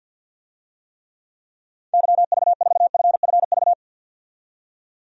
Morse code, heard as a keyed tone, about 48 wpm, 700 Hz. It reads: YV4FFV
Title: DXCC-100 Call Signs - 6 Characters - 50wpm